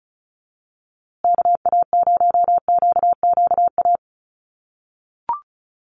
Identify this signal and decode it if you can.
Morse code, heard as a keyed tone, about 34 wpm, 700 Hz. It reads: KA0QQA
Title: USA Call Signs Repeated 3x 35wpm